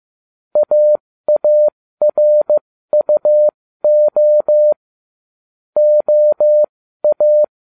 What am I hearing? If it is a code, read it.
Morse code, heard as a keyed tone, about 15 wpm, spaced as for 13 wpm, 600 Hz. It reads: AARUO OA